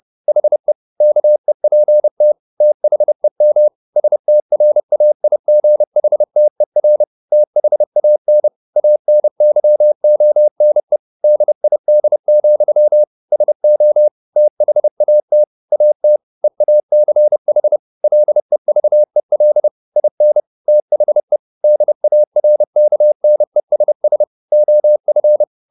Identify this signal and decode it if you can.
Morse code, heard as a keyed tone, about 30 wpm, 600 Hz. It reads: HE KEPT THEM STRAIGHTER THAN ANYONE DID, SO THAT AT EACH LEVEL IN THE DARKNESS OF
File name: the old man and the sea / the old man and the sea 1 30wpm